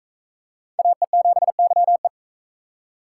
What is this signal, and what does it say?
Morse code, heard as a keyed tone, about 40 wpm, 700 Hz. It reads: AE7YE